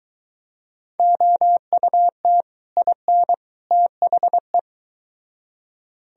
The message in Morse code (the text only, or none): OUT IN THE